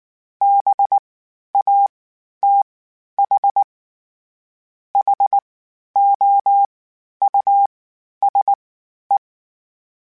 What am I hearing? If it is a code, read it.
Morse code, heard as a keyed tone, about 19 wpm, 800 Hz. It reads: B A T H H O U S E
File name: General / General-mp3s / Houses @18